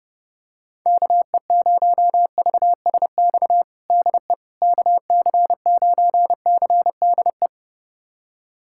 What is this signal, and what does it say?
Morse code, heard as a keyed tone, about 32 wpm, 700 Hz. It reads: KE0VSX DE KC9CDE
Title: USA Callsign DE Callsign 30wpm